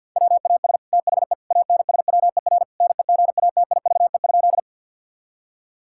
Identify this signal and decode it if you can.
Morse code, heard as a keyed tone, about 50 wpm, 700 Hz. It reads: WAS THE ANSWER NEGATIVE?
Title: Sentences from Top 1000 Words - Review Part 6 - 50wpm